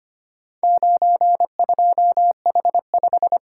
Morse code, heard as a keyed tone, about 25 wpm, 700 Hz. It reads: 92H5
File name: practice / 25wpm / Four Letter-Number Combinations - 25wpm